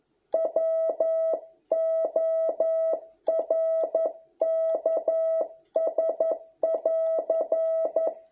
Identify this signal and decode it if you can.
Morse code, heard as a keyed tone, about 11 wpm, 650 Hz. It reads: WORKS <AR>